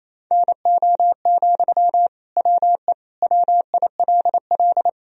Morse code, heard as a keyed tone, about 28 wpm, 700 Hz. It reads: NO, WE WILL
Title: Sentences from Top 1000 Words - Review Part 3 - 28wpm